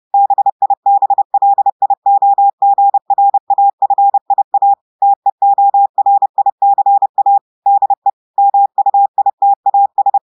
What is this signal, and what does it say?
Morse code, heard as a keyed tone, about 30 wpm, 800 Hz. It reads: BIBLIOGRAFIA TEORICA DE MUITAS
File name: audios / portugais / portugais texte en clair 1 vitesse 30 mots minutes